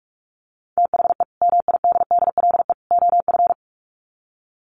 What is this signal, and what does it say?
Morse code, heard as a keyed tone, about 44 wpm, 700 Hz. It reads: THE MIDDLE OF
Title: Sets of 3 Words 45wpm